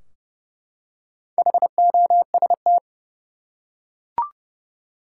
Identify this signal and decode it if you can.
Morse code, heard as a keyed tone, about 30 wpm, 700 Hz. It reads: HOST
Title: Four Letter Words - 30wpm